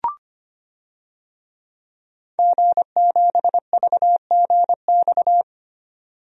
Morse code, heard as a keyed tone, about 25 wpm, 700 Hz. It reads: G7VGX
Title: DXCC-100 Call Signs - 5 Characters - 25wpm